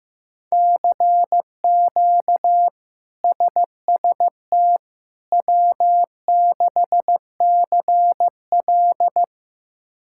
Morse code, heard as a keyed tone, about 15 wpm, 700 Hz. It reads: CQ SST W6CL
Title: K1USN SST QSO Practice - 15wpm